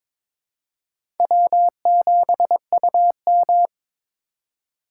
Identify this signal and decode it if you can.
Morse code, heard as a keyed tone, about 22 wpm, 700 Hz. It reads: W7UM
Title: USA Call Signs - 4 Characters - 3x Character Spacing - 22wpm